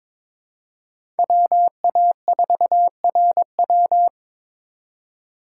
Morse code, heard as a keyed tone, about 22 wpm, 700 Hz. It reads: WA4RW
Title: USA Call Signs - 5 Characters - 2x Character Spacing - 22wpm